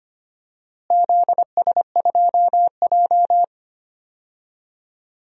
Morse code, heard as a keyed tone, about 25 wpm, 700 Hz. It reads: ZS2J
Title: DXCC-100 Call Signs - 4 Characters - Repeated 3x - 25wpm